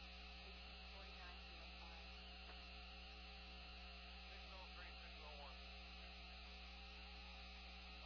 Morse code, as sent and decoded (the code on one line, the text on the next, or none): none